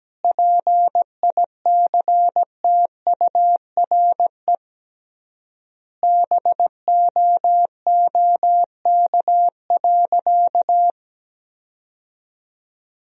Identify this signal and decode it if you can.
Morse code, heard as a keyed tone, about 17 wpm, 700 Hz. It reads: PICTURE BOOK.
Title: 3x Word-Spacing - Sentences from Top 500 Words - 17wpm